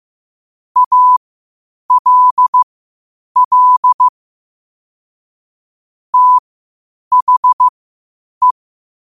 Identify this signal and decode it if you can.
Morse code, heard as a keyed tone, about 15 wpm, spaced as for 8 wpm, 1000 Hz. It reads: ALL THE
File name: qotdfiles / 8wpm